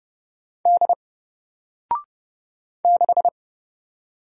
Morse code, heard as a keyed tone, about 30 wpm, 700 Hz. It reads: D 6
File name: Single-Letter-Number-30wpm